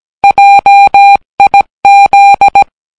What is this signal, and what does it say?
Morse code, heard as a keyed tone, about 17 wpm, 800 Hz. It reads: JIZ